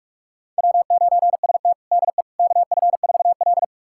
Morse code, heard as a keyed tone, about 42 wpm, 700 Hz. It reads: W9ST DE KF4L
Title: USA Callsign DE Callsign - 4 Characters - 45wpm